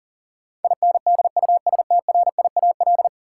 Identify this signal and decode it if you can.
Morse code, heard as a keyed tone, about 40 wpm, 700 Hz. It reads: INDUSTRIAL
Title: Sets of 5 Words - New Word Drill - 40wpm